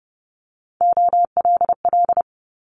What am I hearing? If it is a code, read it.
Morse code, heard as a keyed tone, about 30 wpm, 700 Hz. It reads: OLL
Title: Multi-letter Phonemes Quick Fire 30wpm